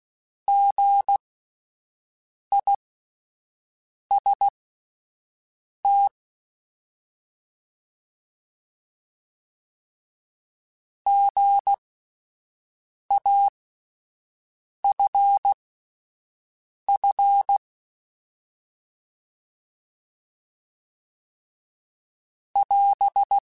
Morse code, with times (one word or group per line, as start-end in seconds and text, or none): 0.5-6.1 GIST
11.1-17.6 GAFF
22.6-23.4 <AS>